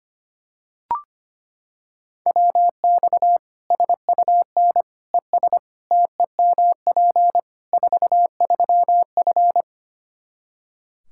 Morse code, heard as a keyed tone, about 25 wpm, 700 Hz. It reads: WX SUN ES TEMP 43F
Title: Practice QSO - Exchange 2 - Info - 25wpm